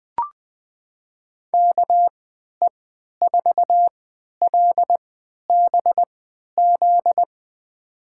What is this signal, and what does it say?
Morse code, heard as a keyed tone, about 20 wpm, spaced as for 11 wpm, 700 Hz. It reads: KE4LBZ
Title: USA Call Signs - 6 Characters - 3x Character Spacing - 20wpm